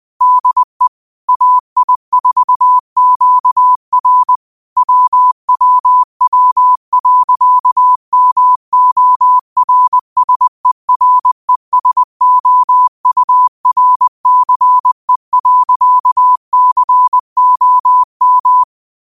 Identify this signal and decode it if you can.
Morse code, heard as a keyed tone, about 20 wpm, 1000 Hz. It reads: DE AI4QR WWW.MORSERESOURCE.COM